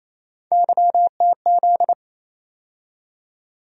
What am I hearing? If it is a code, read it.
Morse code, heard as a keyed tone, about 28 wpm, 700 Hz. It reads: YTZ
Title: USA Call Sign Postfixes - 3 Letters 28wpm